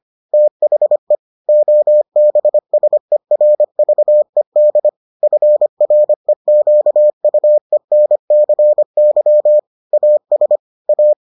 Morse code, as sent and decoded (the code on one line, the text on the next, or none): - .... / --- -... ... .-. ...- . -.. / ..-. .-. --.- ..- . -. -.-. -.-- / .- ... / .-
THE OBSERVED FREQUENCY AS A